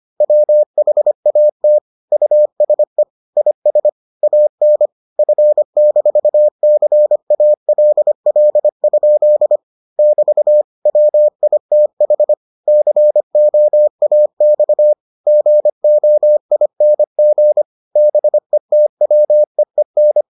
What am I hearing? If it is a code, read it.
Morse code, heard as a keyed tone, about 25 wpm, 600 Hz. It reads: WHAT USE IS AN F-CALL? <BT> WITH COAX GOING BETWEEN